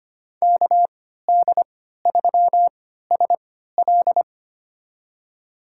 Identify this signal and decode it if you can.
Morse code, heard as a keyed tone, about 25 wpm, spaced as for 14 wpm, 700 Hz. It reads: KD3SL
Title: USA Call Signs - 5 Characters - 3x Character Spacing - 25wpm